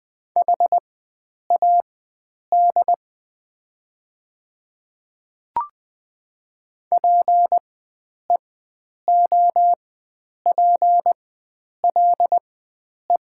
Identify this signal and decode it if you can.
Morse code, heard as a keyed tone, about 20 wpm, spaced as for 9 wpm, 700 Hz. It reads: HAD PEOPLE
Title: 4x Character Spacing - Top 100 Words - 20wpm